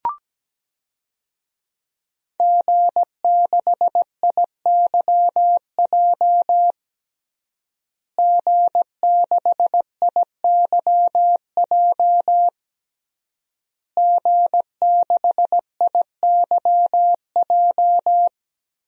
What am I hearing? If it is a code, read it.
Morse code, heard as a keyed tone, about 17 wpm, 700 Hz. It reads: G6IYJ G6IYJ G6IYJ
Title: DXCC-100 Call Signs - 5 Characters - Repeated 3x - 17wpm